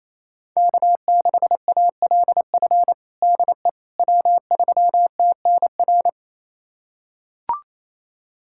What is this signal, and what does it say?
Morse code, heard as a keyed tone, about 28 wpm, 700 Hz. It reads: K6ALF DE W3TNR
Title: USA Callsign DE Callsign - 5 Characters - 28wpm